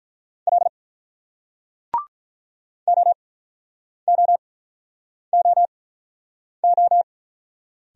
Morse code, heard as a keyed tone, about 48 wpm, spaced as for 9 wpm, 700 Hz. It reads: L OOOO